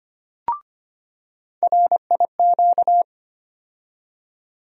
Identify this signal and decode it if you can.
Morse code, heard as a keyed tone, about 25 wpm, 700 Hz. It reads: RIQ
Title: USA Call Sign Postfixes - 3 Letters 25wpm